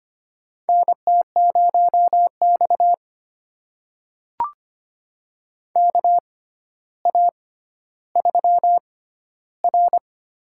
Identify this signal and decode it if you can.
Morse code, heard as a keyed tone, about 25 wpm, spaced as for 13 wpm, 700 Hz. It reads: NT0X KA3R
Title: USA Call Signs - 4 Characters - 6x Character Spacing - 25wpm